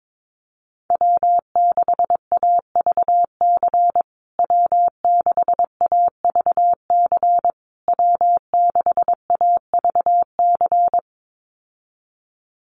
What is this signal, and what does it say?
Morse code, heard as a keyed tone, about 22 wpm, 700 Hz. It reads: W6AVC W6AVC W6AVC
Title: USA Call Signs Repeated 3x 22wpm